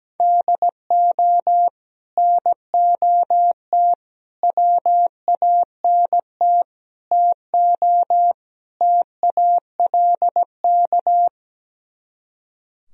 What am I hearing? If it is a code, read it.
Morse code, heard as a keyed tone, about 17 wpm, 700 Hz. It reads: DO NOT WANT TO TALK
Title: Speed-Racing - Sets of 5 Words 17wpm